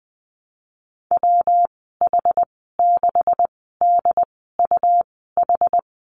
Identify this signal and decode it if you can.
Morse code, heard as a keyed tone, about 20 wpm, spaced as for 15 wpm, 700 Hz. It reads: WH6DUH
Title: 179 Lesson 29g - Slash - Call Signs 20wpm